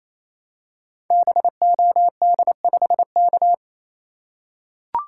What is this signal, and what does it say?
Morse code, heard as a keyed tone, about 28 wpm, 700 Hz. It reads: BOD5K
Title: Code Groups - 5 Letter:Numbers - 28wpm